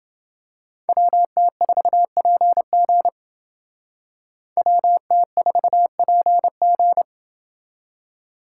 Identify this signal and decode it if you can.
Morse code, heard as a keyed tone, about 29 wpm, 700 Hz. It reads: WT4PG WT4PG